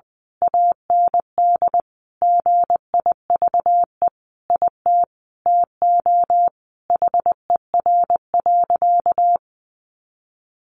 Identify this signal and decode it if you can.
Morse code, heard as a keyed tone, about 20 wpm, 700 Hz. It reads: AND GIVE IT TO HER.
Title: Sentences from Top 1400 Words 20wpm